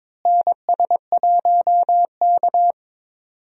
Morse code, heard as a keyed tone, about 22 wpm, 700 Hz. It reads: NS1K
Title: USA Call Signs - 4 Characters - Repeated 3x - 22wpm